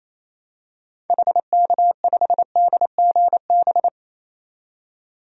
Morse code, heard as a keyed tone, about 28 wpm, 700 Hz. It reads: HK5DGB
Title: DXCC-100 Call Signs - 6 Characters - 28wpm